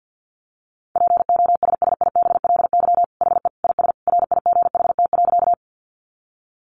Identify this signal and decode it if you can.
Morse code, heard as a keyed tone, about 50 wpm, 700 Hz. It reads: POSSIBLY HE IS RIGHT.
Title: Sentences from Top 1300 Words 50wpm